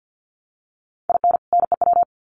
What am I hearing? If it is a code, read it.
Morse code, heard as a keyed tone, about 50 wpm, 700 Hz. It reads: IN NEW
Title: Sets of 2 Words 50wpm